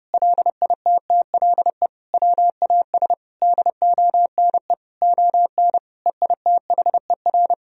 Morse code, heard as a keyed tone, about 30 wpm, 700 Hz. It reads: LITTLE WAS DONE ON EITHER
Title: Sentences from Top 1000 Words - Review Part 4 - 30wpm